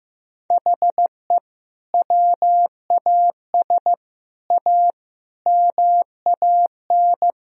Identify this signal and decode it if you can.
Morse code, heard as a keyed tone, about 15 wpm, 700 Hz. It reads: HE WAS A MAN
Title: Sentences from Top 1000 Words - Review Part 3 - 15wpm